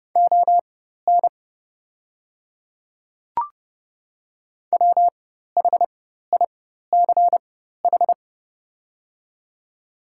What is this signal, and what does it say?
Morse code, heard as a keyed tone, about 29 wpm, spaced as for 14 wpm, 700 Hz. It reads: ON WHICH
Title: 4x Character Spacing - Top 100 Words - 30wpm